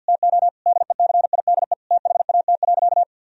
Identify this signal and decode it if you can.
Morse code, heard as a keyed tone, about 49 wpm, 700 Hz. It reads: TO DECIDE THAT.
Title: Sentences from Top 1000 Words - Review Part 3 - 50wpm